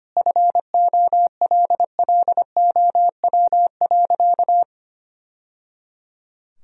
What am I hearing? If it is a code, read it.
Morse code, heard as a keyed tone, about 25 wpm, 700 Hz. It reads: FOLLOW.
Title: Sentences from Top 1000 Words - Review Part 3 - 25wpm